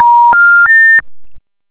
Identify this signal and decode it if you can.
Special information tone.